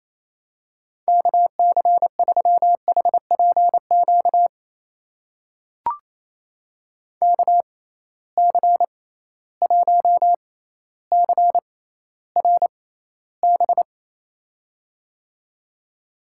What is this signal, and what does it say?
Morse code, heard as a keyed tone, about 27 wpm, spaced as for 14 wpm, 700 Hz. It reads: KC3HPQ KC1CRB
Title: USA Call Signs - 6 Characters - 6x Character Spacing - 28wpm